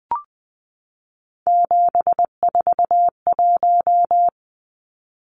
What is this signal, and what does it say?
Morse code, heard as a keyed tone, about 20 wpm, 700 Hz. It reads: 741
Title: Numbers - 3-Digit 20wpm